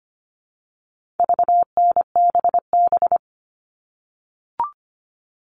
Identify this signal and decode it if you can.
Morse code, heard as a keyed tone, about 25 wpm, 700 Hz. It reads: VNBB
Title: All Cut Numbers - 4-Digit 25wpm